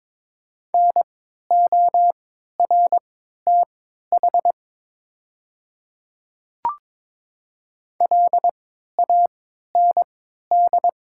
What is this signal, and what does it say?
Morse code, heard as a keyed tone, about 22 wpm, spaced as for 13 wpm, 700 Hz. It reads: NORTH LAND
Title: 3x Character Spacing - Top 500 Words - 22wpm